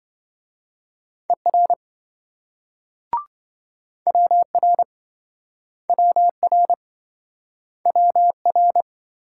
Morse code, heard as a keyed tone, about 26 wpm, 700 Hz. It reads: ER WR WR WR